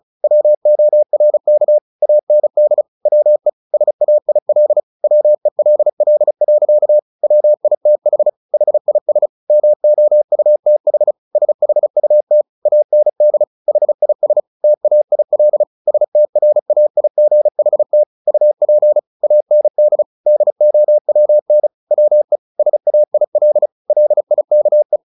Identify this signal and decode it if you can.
Morse code, heard as a keyed tone, about 34 wpm, 600 Hz. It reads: WORK AND WE SAIL WELL. WITH HIS MOUTH SHUT AND HIS TAIL STRAIGHT UP AND DOWN WE SAIL LIKE
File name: the old man and the sea 2 35wpm